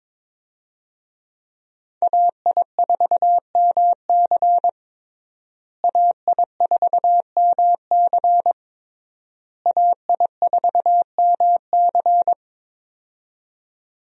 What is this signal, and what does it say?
Morse code, heard as a keyed tone, about 22 wpm, 700 Hz. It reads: AI4MC AI4MC AI4MC
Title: USA Call Signs - 5 Characters - Repeated 3x - 22wpm